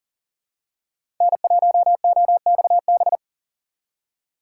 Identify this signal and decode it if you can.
Morse code, heard as a keyed tone, about 40 wpm, 700 Hz. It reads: N1OXB
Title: USA Call Signs - 5 Characters - 6x Character Spacing - 40wpm